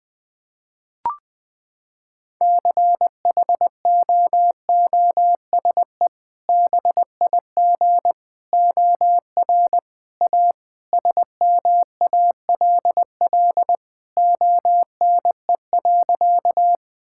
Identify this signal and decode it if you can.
Morse code, heard as a keyed tone, about 20 wpm, 700 Hz. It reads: CHOOSE BIG OR A SMALL ONE.
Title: Sentences with 2-Word Phrases - 20wpm